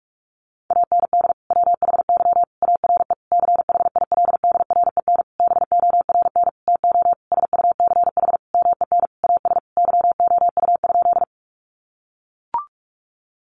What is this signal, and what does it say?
Morse code, heard as a keyed tone, about 44 wpm, 700 Hz. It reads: AND WHY ARE CHILDREN BORN TO SUCH MEN AS YOU?
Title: Sentences from Top 1000 Words 45wpm